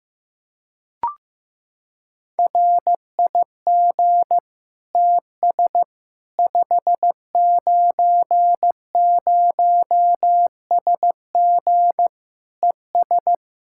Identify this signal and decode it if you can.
Morse code, heard as a keyed tone, about 15 wpm, 700 Hz. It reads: RIG TS 590SG ES
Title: Practice QSO - Exchange 2 - Info - 15wpm